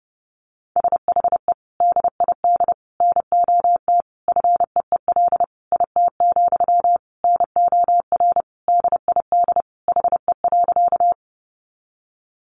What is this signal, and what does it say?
Morse code, heard as a keyed tone, about 30 wpm, 700 Hz. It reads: SHE DID NOT FEEL IT, NOR DID HE.
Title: Sentences from Top 1500 Words 30wpm